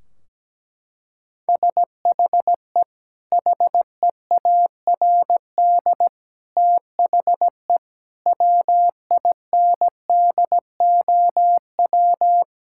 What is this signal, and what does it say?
Morse code, heard as a keyed tone, about 17 wpm, 700 Hz. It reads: SHE HEARD THE WINDOW
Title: Sentences from Top 1600 Words 17wpm